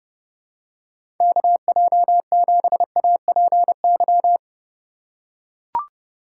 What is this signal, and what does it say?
Morse code, heard as a keyed tone, about 30 wpm, 700 Hz. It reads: KJ7APY
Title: USA Call Signs - 4x Character Spacing - 30wpm